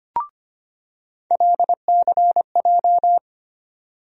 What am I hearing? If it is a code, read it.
Morse code, heard as a keyed tone, about 25 wpm, 700 Hz. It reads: LCJ
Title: USA Call Sign Postfixes - 3 Letters 25wpm